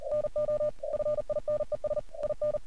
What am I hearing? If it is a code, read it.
Morse code, heard as a keyed tone, about 37 wpm, 600 Hz. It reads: NO FINES IN
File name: archived / CRSnet